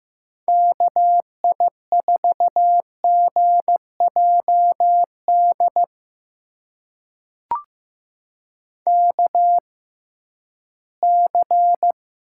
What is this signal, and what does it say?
Morse code, heard as a keyed tone, about 15 wpm, spaced as for 11 wpm, 700 Hz. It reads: KI4GJD KC
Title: USA Call Signs - 6 Characters - 6x Character Spacing - 15wpm